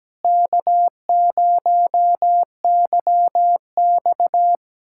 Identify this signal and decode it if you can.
Morse code, heard as a keyed tone, about 17 wpm, 700 Hz. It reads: K0YX
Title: USA Callsign DE Callsign - 4 Characters - 17wpm